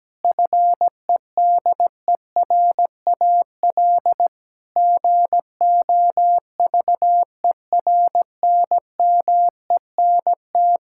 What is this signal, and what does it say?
Morse code, heard as a keyed tone, about 17 wpm, 700 Hz. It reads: FEDERAL GOVERNMENT
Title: Sentences from Top 1000 Words - Review Part 6 - 17wpm